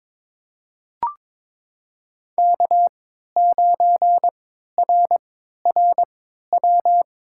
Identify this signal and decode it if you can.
Morse code, heard as a keyed tone, about 22 wpm, spaced as for 12 wpm, 700 Hz. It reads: K9RRW
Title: USA Call Signs - 5 Characters - 3x Character Spacing - 22wpm